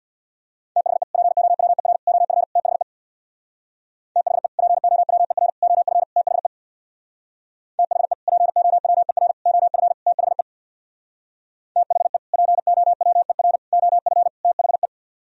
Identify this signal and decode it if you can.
Morse code, heard as a keyed tone, about 62 wpm, 690 Hz. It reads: THE POWER OF THE THE POWER OF THE THE POWER OF THE THE POWER OF THE
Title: Speed-Racing - Sets of 4 Words 50wpm